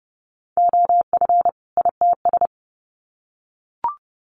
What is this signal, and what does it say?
Morse code, heard as a keyed tone, about 30 wpm, 700 Hz. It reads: OF ITS
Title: Sets of 2 Words 30wpm